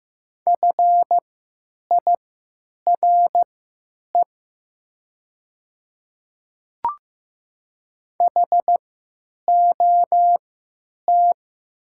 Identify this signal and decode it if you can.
Morse code, heard as a keyed tone, about 15 wpm, spaced as for 9 wpm, 700 Hz. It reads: FIRE HOT
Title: 3x Character Spacing - Top 500 Words - 15wpm